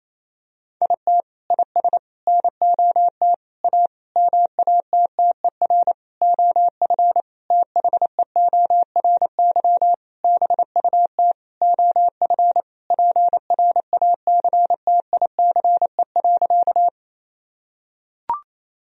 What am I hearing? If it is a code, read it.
Morse code, heard as a keyed tone, about 28 wpm, 700 Hz. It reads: IT IS NOT A MATTER OF THEORY BUT OF PRACTICE.
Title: Sentences from Top 1000 Words - Review Part 4 - 28wpm